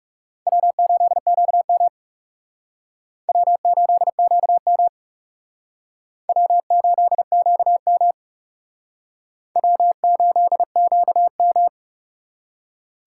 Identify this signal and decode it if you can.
Morse code, heard as a keyed tone, about 38 wpm, 700 Hz. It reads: W8QM W8QM W8QM W8QM